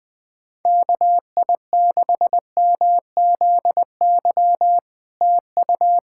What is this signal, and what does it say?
Morse code, heard as a keyed tone, about 20 wpm, 700 Hz. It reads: KI6MZY TU
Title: SOTA QSO Practice 20wpm